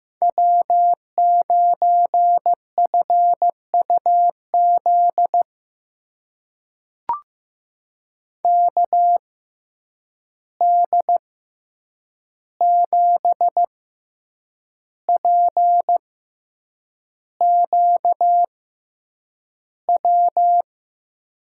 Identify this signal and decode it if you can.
Morse code, heard as a keyed tone, about 15 wpm, spaced as for 7 wpm, 700 Hz. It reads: W9FUZ KD7PQW
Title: USA Call Signs - 6x Character Spacing - 15wpm